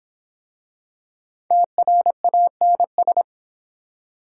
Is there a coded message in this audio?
Yes: Morse code from 1.5 to 3.2 s, heard as a keyed tone, about 23 wpm, 700 Hz. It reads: TRANS